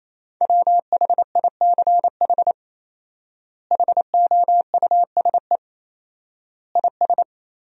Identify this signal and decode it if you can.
Morse code, heard as a keyed tone, about 28 wpm, 700 Hz. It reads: WHICH HOUSE IS
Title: 4x Word-Spacing - Sentences from Top 200 Words - 28wpm